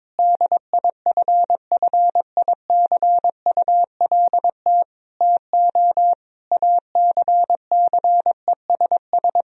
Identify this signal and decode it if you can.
Morse code, heard as a keyed tone, about 22 wpm, 700 Hz. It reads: DIFFICULT TO ACCESS
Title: Sentences from Top 1000 Words - Review Part 4 - 22wpm